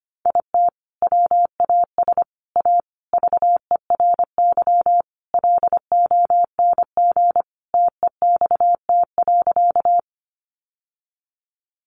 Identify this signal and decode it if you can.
Morse code, heard as a keyed tone, about 25 wpm, 700 Hz. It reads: IT WAS A VERY LONG TEXT.
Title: Sentences from Top 1200 Words 25wpm